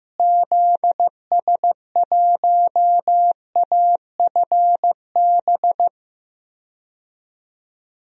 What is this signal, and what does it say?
Morse code, heard as a keyed tone, about 15 wpm, 700 Hz. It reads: ZS1AFB
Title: DXCC-100 Call Signs - 6 Characters - Repeated 3x - 15wpm